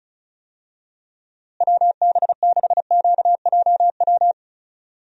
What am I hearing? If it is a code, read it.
Morse code, heard as a keyed tone, about 34 wpm, 700 Hz. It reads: WB6QJW
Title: USA Call Signs - 6 Characters - Repeated 3x - 35wpm